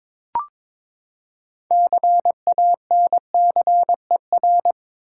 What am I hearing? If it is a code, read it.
Morse code, heard as a keyed tone, about 22 wpm, 700 Hz. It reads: CANCER
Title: Six Letter Words - 22wpm